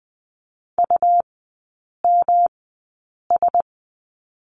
Morse code, heard as a keyed tone, about 20 wpm, spaced as for 8 wpm, 700 Hz. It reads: UMS
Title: 127 Lesson 21e - Number 6 - Three Characters 20wpm